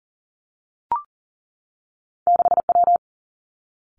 Morse code, heard as a keyed tone, about 40 wpm, 700 Hz. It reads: 6W
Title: Two Challenging Letter-Number Combinations 40wpm